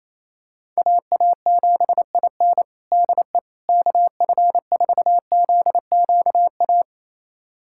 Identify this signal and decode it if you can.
Morse code, heard as a keyed tone, about 28 wpm, 700 Hz. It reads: AA7IN DE KF4ZQA
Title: USA Callsign DE Callsign 28wpm